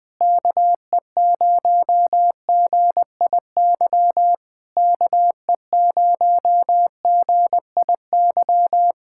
Morse code, heard as a keyed tone, about 20 wpm, 700 Hz. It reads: KE0GIY KE0GIY